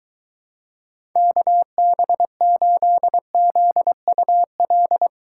Morse code, heard as a keyed tone, about 23 wpm, 700 Hz. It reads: KB8ZUL